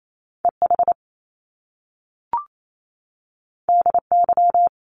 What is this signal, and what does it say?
Morse code, heard as a keyed tone, about 28 wpm, 700 Hz. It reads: E H D Y
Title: Top Two Letter Combinations in Words 28wpm